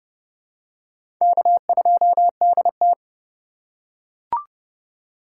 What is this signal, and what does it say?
Morse code, heard as a keyed tone, about 32 wpm, 700 Hz. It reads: K2DT
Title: USA Call Signs - 4 Characters - 4x Character Spacing - 30wpm